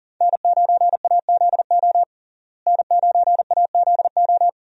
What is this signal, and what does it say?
Morse code, heard as a keyed tone, about 40 wpm, 700 Hz. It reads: N9AZO N9AZO
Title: USA Call Signs - 5 Characters - Repeated 2x - 40wpm